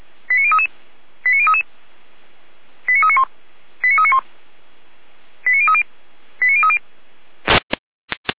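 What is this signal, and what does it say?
Five-tone selective call (ZVEI), heard as quick runs of five tones, tones of about 80 ms. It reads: 89033 89033 89391 89391 89030 89030
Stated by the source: ZVEI five-tone selective call, address 89050, received by SDR.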